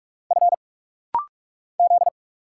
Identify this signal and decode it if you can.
Morse code, heard as a keyed tone, about 46 wpm, 700 Hz. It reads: F Z
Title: Single Letter-Number Alt - Rapid-Fire 45wpm